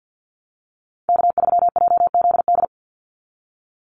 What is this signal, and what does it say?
Morse code, heard as a keyed tone, about 50 wpm, 700 Hz. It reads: K3JZD
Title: NA SOTA Chaser Callsigns 50wpm